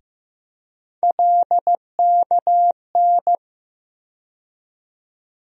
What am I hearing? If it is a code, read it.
Morse code, heard as a keyed tone, about 15 wpm, 700 Hz. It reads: LKN
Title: USA Call Sign Postfixes - 3 Letters 15wpm